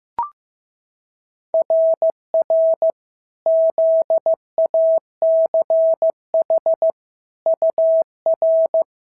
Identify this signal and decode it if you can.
Morse code, heard as a keyed tone, about 14 wpm, 650 Hz. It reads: RR ZACH UR
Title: SOTA QSO Practice 15wpm